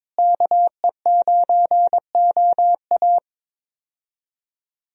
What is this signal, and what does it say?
Morse code, heard as a keyed tone, about 22 wpm, 700 Hz. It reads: KE9OA
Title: USA Call Signs - 5 Characters - Repeated 3x - 22wpm